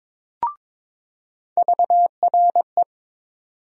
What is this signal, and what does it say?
Morse code, heard as a keyed tone, about 22 wpm, 700 Hz. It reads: VRE
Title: USA Call Sign Postfixes - 3 Letters 22wpm